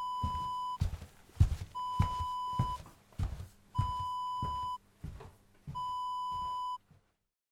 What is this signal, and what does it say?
There is no code.